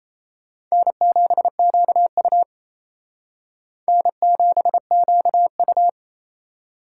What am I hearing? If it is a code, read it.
Morse code, heard as a keyed tone, about 32 wpm, 700 Hz. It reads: N7QU N7QU
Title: Speed Racing - USA 1x2 Call Signs 28wpm